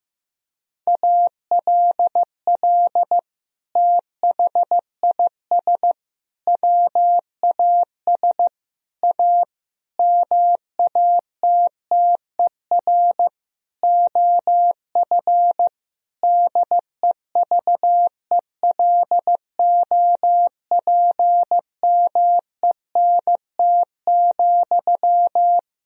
Morse code, heard as a keyed tone, about 15 wpm, 700 Hz. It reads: ALL THIS WAS A MATTER OF DEVELOPMENT,
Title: Sentences from Top 1000 Words - Review Part 4 - 15wpm